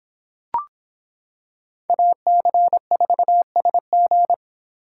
Morse code, heard as a keyed tone, about 25 wpm, 700 Hz. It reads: AC4SG